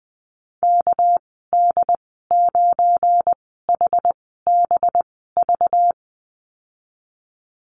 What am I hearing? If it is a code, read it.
Morse code, heard as a keyed tone, about 20 wpm, spaced as for 14 wpm, 700 Hz. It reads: KD9HBV
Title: 231 Lesson 37g - Letter x - Call Signs 20wpm